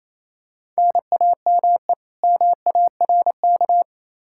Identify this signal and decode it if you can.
Morse code, heard as a keyed tone, about 28 wpm, 700 Hz. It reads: NAME MARK
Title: Practice QSO - Exchange 1 - Beginning - 28wpm